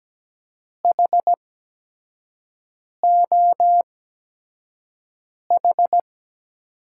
Morse code, heard as a keyed tone, about 17 wpm, spaced as for 5 wpm, 700 Hz. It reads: HOH
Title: USA Call Signs - 6 Characters - 8x Character Spacing - 17wpm